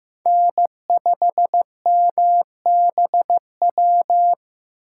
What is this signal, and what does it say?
Morse code, heard as a keyed tone, about 15 wpm, 700 Hz. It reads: N5MBW
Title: USA Call Signs - 8x Character Spacing - 15wpm